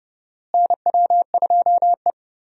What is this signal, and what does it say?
Morse code, heard as a keyed tone, about 32 wpm, 700 Hz. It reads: NW2E
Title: USA Call Signs - 4 Characters - Repeated 3x - 30wpm